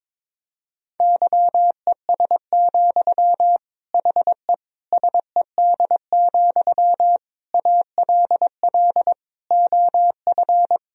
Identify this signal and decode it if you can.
Morse code, heard as a keyed tone, about 22 wpm, 700 Hz. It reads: YES, HE SED, ALL OF
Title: Sentences from Top 500 Words and Abbreviations 22wpm